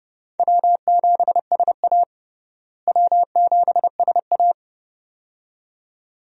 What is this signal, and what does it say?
Morse code, heard as a keyed tone, about 30 wpm, 700 Hz. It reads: W7SA W7SA
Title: USA Call Signs - 4 Characters - Repeated 2x - 30wpm